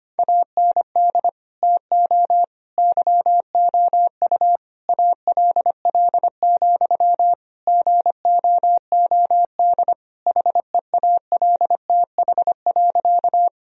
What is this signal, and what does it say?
Morse code, heard as a keyed tone, about 25 wpm, 700 Hz. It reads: AND TO YOU ALL, GOOD HEALTH.
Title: Sentences from Top 1000 Words - Review Part 3 - 25wpm